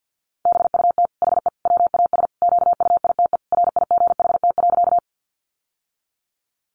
Morse code, heard as a keyed tone, about 50 wpm, 700 Hz. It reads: BUT HE WAS QUITE RIGHT.
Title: Sentences from Top 700 Words 50wpm